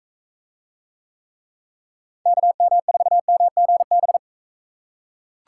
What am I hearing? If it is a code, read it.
Morse code, heard as a keyed tone, about 43 wpm, 700 Hz. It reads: KM4MGB